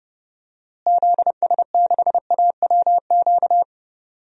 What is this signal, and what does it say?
Morse code, heard as a keyed tone, about 30 wpm, 700 Hz. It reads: ZS6AWQ
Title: DXCC-100 Call Signs - 6 Characters - Repeated 3x - 30wpm